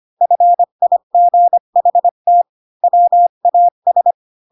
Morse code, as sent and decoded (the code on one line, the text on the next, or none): ..-. .. --. .... - / .-- .- ...
FIGHT WAS